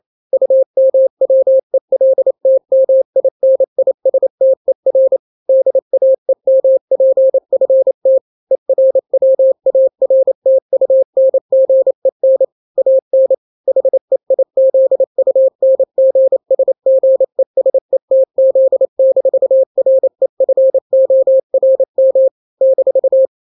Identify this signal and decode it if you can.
Morse code, heard as a keyed tone, about 28 wpm, 530 Hz. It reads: UMWELTMINISTER DAEMPFT ERWARTUNGEN AN HEIZUNGSGESETZ-REFORM -
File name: daily / 27wpm